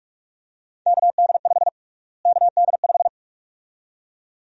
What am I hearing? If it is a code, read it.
Morse code, heard as a keyed tone, about 46 wpm, 700 Hz. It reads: KD5 KD5
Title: USA Call Sign Prefixes - Repeated 2x - 45wpm